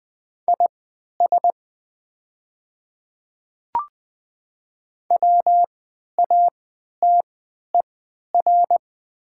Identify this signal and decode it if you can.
Morse code, heard as a keyed tone, about 20 wpm, spaced as for 11 wpm, 700 Hz. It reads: IS WATER
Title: 3x Character Spacing - Top 100 Words - 20wpm